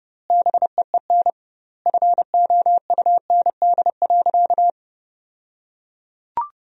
Morse code, heard as a keyed tone, about 30 wpm, 700 Hz. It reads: BEEN FOUND.
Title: 2x Word-Spacing - Sentences from Top 300 Words - 30wpm